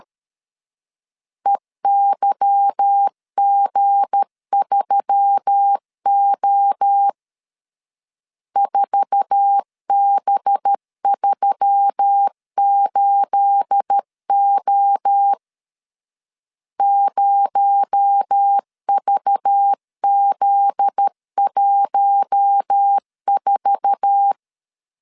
Morse code, heard as a keyed tone, about 13 wpm, 800 Hz. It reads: EYG3O 4B38O 0VZ14
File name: CW / DET / 1 9 1500C S B 12wpm 3819